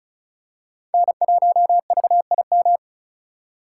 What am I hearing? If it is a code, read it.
Morse code, heard as a keyed tone, about 36 wpm, 700 Hz. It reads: N1VIM
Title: USA Call Signs - 5 Characters - 2x Character Spacing - 35wpm